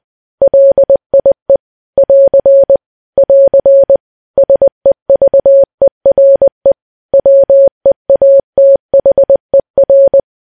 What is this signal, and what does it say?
Morse code, heard as a keyed tone, about 20 wpm, 550 Hz. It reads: LIE <AR> <AR> SEVERE WEATHER
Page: news-headlines-in-morse-code-at-20-wpm